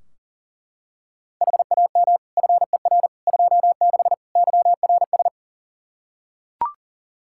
Morse code, heard as a keyed tone, about 40 wpm, 700 Hz. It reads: HAM FER 26 YRS
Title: Practice QSO - Exchange 3 - More Info - 40wpm